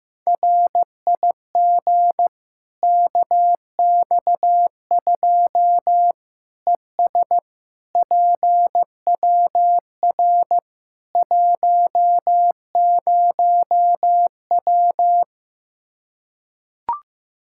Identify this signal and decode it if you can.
Morse code, heard as a keyed tone, about 16 wpm, 700 Hz. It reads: RIG KX2 ES PWR 10W
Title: K1USN SST QSO Practice 15wpm